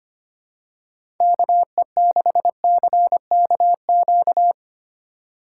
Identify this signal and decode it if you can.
Morse code, heard as a keyed tone, about 25 wpm, 700 Hz. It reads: KE6CKQ
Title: USA Call Signs - 6 Characters - 2x Character Spacing - 25wpm